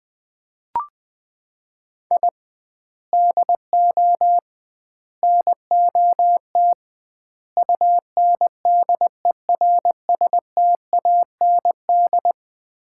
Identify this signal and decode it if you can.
Morse code, heard as a keyed tone, about 20 wpm, 700 Hz. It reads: I DO NOT UNDERSTAND
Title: 2x Word-Spacing - Sets of 4 Words - 20wpm